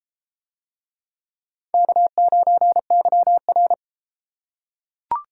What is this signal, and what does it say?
Morse code, heard as a keyed tone, about 33 wpm, 690 Hz. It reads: K9YR